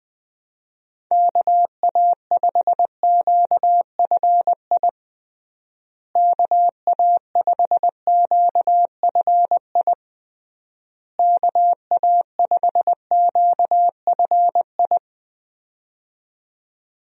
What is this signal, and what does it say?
Morse code, heard as a keyed tone, about 20 wpm, 700 Hz. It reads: KA5QFI KA5QFI KA5QFI
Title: USA Call Signs - 6 Characters - Repeated 3x - 20wpm